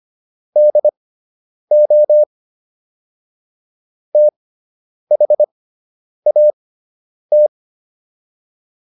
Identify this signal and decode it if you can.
Morse code, heard as a keyed tone, about 25 wpm, spaced as for 9 wpm, 600 Hz. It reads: DO THAT